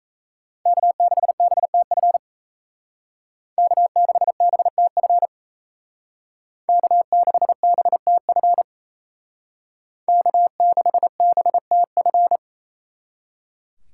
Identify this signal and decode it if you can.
Morse code, heard as a keyed tone, about 35 wpm, 700 Hz. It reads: K6BTF K6BTF K6BTF K6BTF